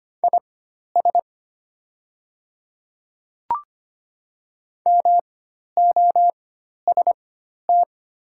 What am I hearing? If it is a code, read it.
Morse code, heard as a keyed tone, about 25 wpm, spaced as for 12 wpm, 690 Hz. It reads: IS MOST
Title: 4x Character Spacing - Top 100 Words - 25wpm